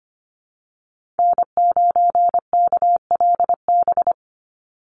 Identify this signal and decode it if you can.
Morse code, heard as a keyed tone, about 25 wpm, 700 Hz. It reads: N9KLB
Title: USA Call Signs 25wpm